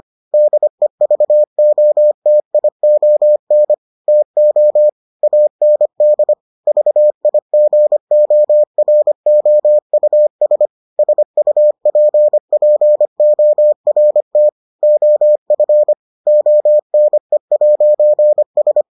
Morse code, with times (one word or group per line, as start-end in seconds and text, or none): 0.3-3.7 DEVOTION
4.1-4.9 TO
5.2-6.3 AND
6.7-10.7 VIGOROUS
11.0-14.5 SUPPORT
14.8-15.9 OF
16.3-18.8 ONE'S